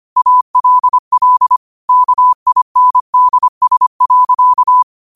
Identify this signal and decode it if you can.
Morse code, heard as a keyed tone, about 25 wpm, 1000 Hz. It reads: ALL KINDS.
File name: qotdfiles / 25wpm